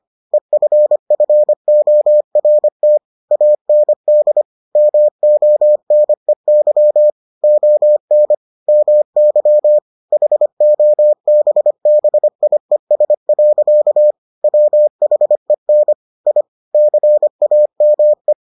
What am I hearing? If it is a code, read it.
Morse code, heard as a keyed tone, about 25 wpm, 600 Hz. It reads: EFFORT AND MONEY ON MY HOBBIES. WHEN I CAME